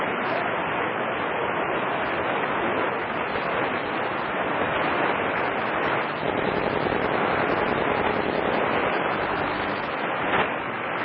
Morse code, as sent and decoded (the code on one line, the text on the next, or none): none